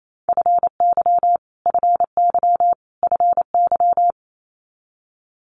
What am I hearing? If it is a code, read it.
Morse code, heard as a keyed tone, about 28 wpm, 700 Hz. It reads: FY FY FY
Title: Multi-letter Phonemes Repeated 3x 28wpm